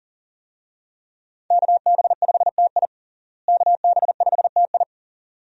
Morse code, heard as a keyed tone, about 37 wpm, 700 Hz. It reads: KB5TI KB5TI
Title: USA Call Signs - 5 Characters - Repeated 2x - 40wpm